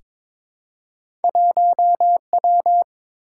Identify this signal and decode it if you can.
Morse code, heard as a keyed tone, about 22 wpm, 700 Hz. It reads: 1W
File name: practice / 22wpm / Two Letter-Number Combinations - 22wpm